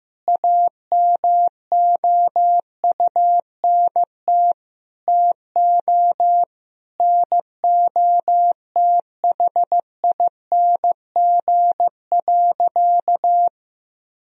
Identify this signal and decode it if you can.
Morse code, heard as a keyed tone, about 16 wpm, 700 Hz. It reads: AMOUNT TO NOTHING.
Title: Sentences from Top 1000 Words - Review Part 4 - 15wpm